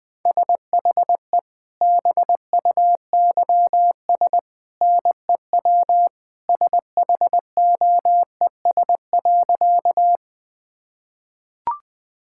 Morse code, heard as a keyed tone, about 20 wpm, 700 Hz. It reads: SHE BUYS NEW SHOES.
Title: Easy Sentences with Hints - 20wpm